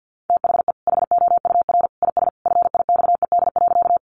Morse code, heard as a keyed tone, about 50 wpm, 700 Hz. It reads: THE HOUR IS FIXED.
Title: Sentences from Top 1100 Words 50wpm